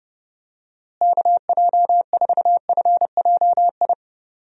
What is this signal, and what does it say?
Morse code, heard as a keyed tone, about 32 wpm, 700 Hz. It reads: KJ4FJI
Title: USA Call Signs - 6 Characters - Repeated 2x - 30wpm